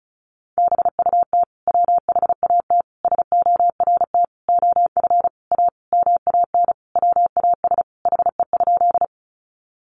Morse code, heard as a keyed tone, about 34 wpm, 700 Hz. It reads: BUT WHAT SORT OF A MAN WAS HE?
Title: Sentences from Top 1300 Words 35wpm